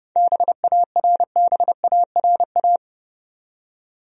Morse code, heard as a keyed tone, about 32 wpm, 700 Hz. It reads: BARBARA
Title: USA Common Names - 30wpm